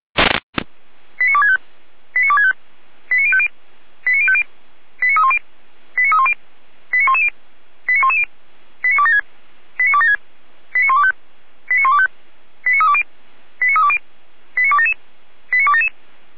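Five-tone selective call (ZVEI), heard as quick runs of five tones, tones of about 70 ms. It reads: 89265 89265 89050 89050 89310 89310 89110 89110 89276 89276 89125 89125 89320 89320 89277 89277
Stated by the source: ZVEI five-tone selective call received by SDR, from the Feuerwehr Orlamünde fire brigade audio server.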